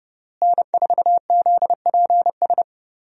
Morse code, heard as a keyed tone, about 32 wpm, 700 Hz. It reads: N4ZPS
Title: USA Call Signs - 5 Characters - Repeated 3x - 30wpm